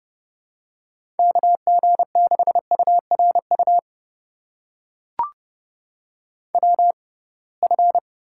Morse code, heard as a keyed tone, about 30 wpm, spaced as for 23 wpm, 700 Hz. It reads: KG6URU WF
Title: USA Call Signs - 6x Character Spacing - 30wpm